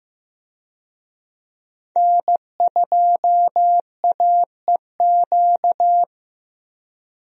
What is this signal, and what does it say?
Morse code, heard as a keyed tone, about 15 wpm, 700 Hz. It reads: N2AEQ
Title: USA Call Signs - 5 Characters - Repeated 3x - 15wpm